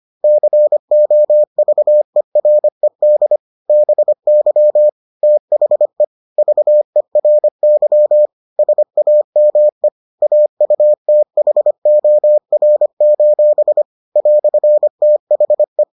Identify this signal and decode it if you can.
Morse code, heard as a keyed tone, about 25 wpm, 600 Hz. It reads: COVERED BY THE VERY SAME AUTHOR: "THE